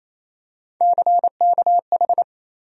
Morse code, heard as a keyed tone, about 28 wpm, 700 Hz. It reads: CKH
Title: USA Call Sign Postfixes - 3 Letters 28wpm